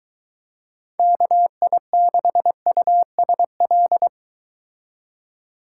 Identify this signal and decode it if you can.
Morse code, heard as a keyed tone, about 23 wpm, 700 Hz. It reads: KI6USL